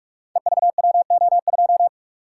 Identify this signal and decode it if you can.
Morse code, heard as a keyed tone, about 44 wpm, 700 Hz. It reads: EVWO2
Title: Code Groups - 5 Letter:Numbers - 45wpm